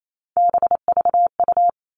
Morse code, heard as a keyed tone, about 28 wpm, 700 Hz. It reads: BVU
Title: All Cut Numbers - 3-Digit 28wpm